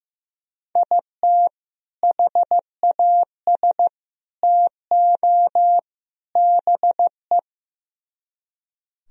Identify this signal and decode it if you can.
Morse code, heard as a keyed tone, about 16 wpm, 700 Hz. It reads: IT HAS TO BE
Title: Speed-Racing - Sets of 4 Words 15wpm